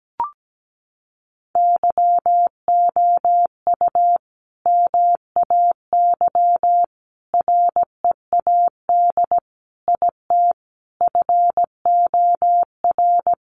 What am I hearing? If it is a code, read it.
Morse code, heard as a keyed tone, about 17 wpm, 700 Hz. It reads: YOU MAY READ IT FOR
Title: Sentences from Top 600 Words 17wpm